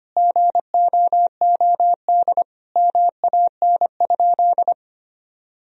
Morse code, heard as a keyed tone, about 25 wpm, 700 Hz. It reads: GOOD MAN?
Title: Sentences from Top 1000 Words - Review Part 4 - 25wpm